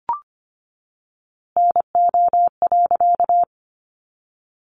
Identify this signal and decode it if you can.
Morse code, heard as a keyed tone, about 25 wpm, 700 Hz. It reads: NO.
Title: A Sound Of Thunder 25wpm